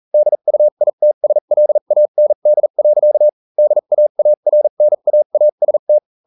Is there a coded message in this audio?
Yes: Morse code from 0.1 to 6.0 s, heard as a keyed tone, about 40 wpm, spaced as for 35 wpm, 600 Hz. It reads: DUITSLAND. DAARNAAST